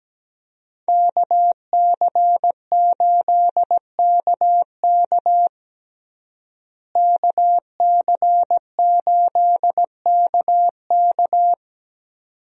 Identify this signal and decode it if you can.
Morse code, heard as a keyed tone, about 17 wpm, 700 Hz. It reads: KC8KK KC8KK
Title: USA Call Signs - 5 Characters - Repeated 2x - 17wpm